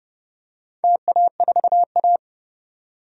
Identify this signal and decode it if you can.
Morse code, heard as a keyed tone, about 30 wpm, 700 Hz. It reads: TA4A
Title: DXCC-100 Call Signs - 4 Characters - 30wpm